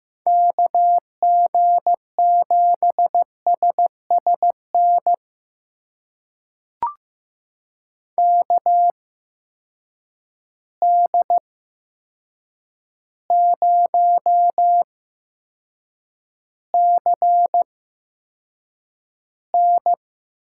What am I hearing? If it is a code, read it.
Morse code, heard as a keyed tone, about 15 wpm, spaced as for 7 wpm, 700 Hz. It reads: KG7SSN KD0CN
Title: USA Call Signs - 6 Characters - 8x Character Spacing - 15wpm